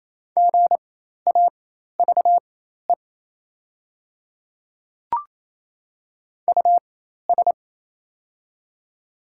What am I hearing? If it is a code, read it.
Morse code, heard as a keyed tone, about 28 wpm, spaced as for 13 wpm, 700 Hz. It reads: GAVE US
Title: 4x Character Spacing - Top 500 Words - 28wpm